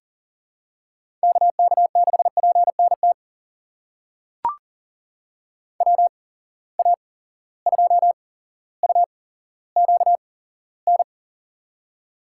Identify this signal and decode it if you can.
Morse code, heard as a keyed tone, about 40 wpm, spaced as for 17 wpm, 700 Hz. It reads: KK6PNT WA2UQN
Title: USA Call Signs - 8x Character Spacing - 40wpm